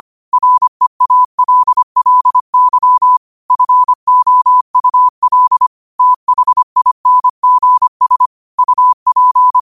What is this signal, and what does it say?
Morse code, heard as a keyed tone, about 25 wpm, 1000 Hz. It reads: REALLY FOUL THINGS UP
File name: qotdfiles / 25wpm